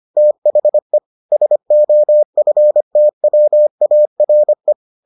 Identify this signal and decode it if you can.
Morse code, heard as a keyed tone, about 25 wpm, 600 Hz. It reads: THE SOFTWARE